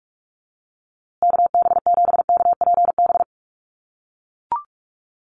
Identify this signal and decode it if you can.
Morse code, heard as a keyed tone, about 44 wpm, 690 Hz. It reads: KB7KPB
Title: USA Call Signs Repeated 3x 45wpm